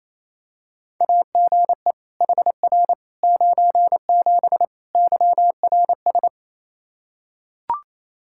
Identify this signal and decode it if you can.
Morse code, heard as a keyed tone, about 28 wpm, 700 Hz. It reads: AGE HR 97 YRS
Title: Practice QSO - Exchange 3 - More Info - 28wpm